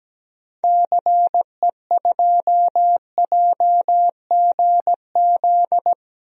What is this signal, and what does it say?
Morse code, heard as a keyed tone, about 17 wpm, 700 Hz. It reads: CE2JGZ
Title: DXCC-100 Call Signs - 6 Characters - 17wpm